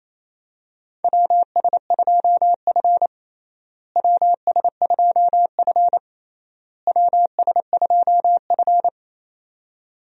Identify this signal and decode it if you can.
Morse code, heard as a keyed tone, about 28 wpm, 700 Hz. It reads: WS2F WS2F WS2F
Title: USA Call Signs - 4 Characters - Repeated 3x - 28wpm